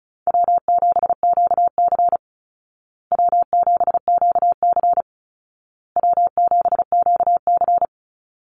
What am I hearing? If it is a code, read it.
Morse code, heard as a keyed tone, about 34 wpm, 700 Hz. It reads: W7QC W7QC W7QC
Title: Call Signs Repeated 2x - 35wpm